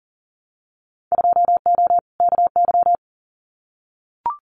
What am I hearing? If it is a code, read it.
Morse code, heard as a keyed tone, about 40 wpm, 700 Hz. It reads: 2O KY
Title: Winter Field Day Exchanges 40wpm